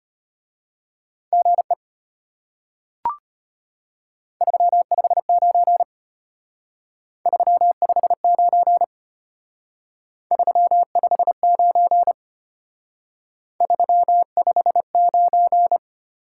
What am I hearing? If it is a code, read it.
Morse code, heard as a keyed tone, about 33 wpm, 700 Hz. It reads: GE 359 359 359 359